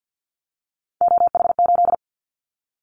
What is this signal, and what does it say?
Morse code, heard as a keyed tone, about 50 wpm, 700 Hz. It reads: OH8
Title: Callsign Prefixes for Top 50 DXCC - 50wpm